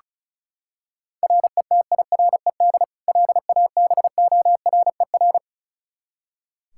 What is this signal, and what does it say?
Morse code, heard as a keyed tone, about 36 wpm, 700 Hz. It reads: RETIRED LABORER
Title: K1USN SST QSO Practice 35wpm